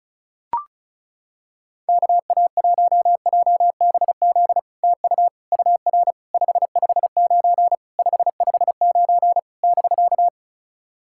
Morse code, heard as a keyed tone, about 36 wpm, 700 Hz. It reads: KA1JBZ TU UR 559 559 <BK>
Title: SOTA QSO Practice 35wpm